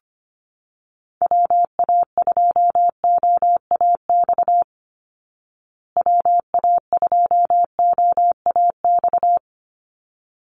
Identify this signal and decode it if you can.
Morse code, heard as a keyed tone, about 25 wpm, 700 Hz. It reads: WA2OAX WA2OAX
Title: Call Signs Repeated 3x - 25wpm